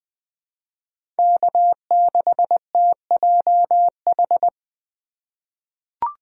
Morse code, heard as a keyed tone, about 20 wpm, 700 Hz. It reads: K6TJH
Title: USA Call Signs - 5 Characters - 2x Character Spacing - 20wpm